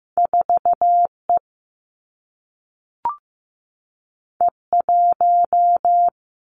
Morse code, heard as a keyed tone, about 15 wpm, 700 Hz. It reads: 4 E E 1
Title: Two Challenging Letter-Number Combinations 15wpm